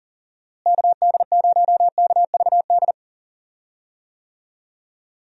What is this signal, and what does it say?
Morse code, heard as a keyed tone, about 40 wpm, 700 Hz. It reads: KD0KVD
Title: USA Call Signs - 6 Characters - Repeated 2x - 40wpm